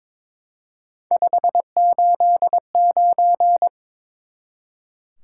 Morse code, heard as a keyed tone, about 21 wpm, 700 Hz. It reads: 589
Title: Top 180 QSO Elements 22wpm